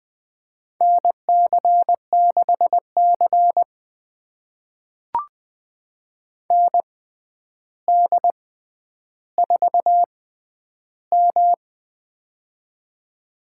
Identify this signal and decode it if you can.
Morse code, heard as a keyed tone, about 20 wpm, spaced as for 10 wpm, 700 Hz. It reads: NC6C ND4M
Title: USA Call Signs - 6x Character Spacing - 20wpm